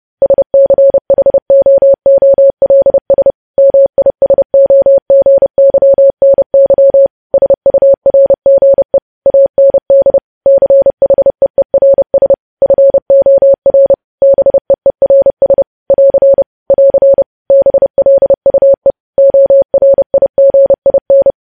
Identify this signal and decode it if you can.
Morse code, heard as a keyed tone, about 30 wpm, 550 Hz. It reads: SCHOOLS MISOGYNY SURGE AND CHEERS FOR BEERS <AR> <AR> BLUE ORIGIN
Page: News Headlines in Morse Code at 30 WPM